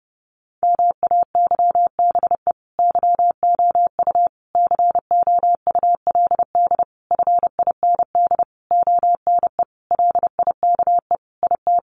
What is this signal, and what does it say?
Morse code, heard as a keyed tone, about 30 wpm, 700 Hz. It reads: MAYBE YOU COULD FIND ONE LIKE IT
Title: Sentences from Top 1400 Words 30wpm